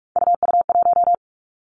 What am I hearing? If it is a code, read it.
Morse code, heard as a keyed tone, about 46 wpm, 700 Hz. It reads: UU1
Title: Three Challenging Letter-Number Combinations 45wpm